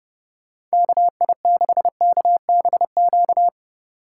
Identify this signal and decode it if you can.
Morse code, heard as a keyed tone, about 31 wpm, 700 Hz. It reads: KI6KBQ